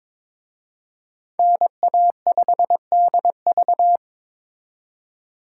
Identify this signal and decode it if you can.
Morse code, heard as a keyed tone, about 26 wpm, 700 Hz. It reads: NA5DV
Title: Speed Racing - USA 2x2 Call Signs 22wpm